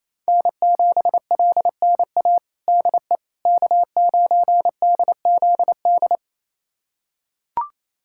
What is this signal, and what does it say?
Morse code, heard as a keyed tone, about 28 wpm, 700 Hz. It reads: N7LNA DE K9DZD
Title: USA Callsign DE Callsign - 5 Characters - 28wpm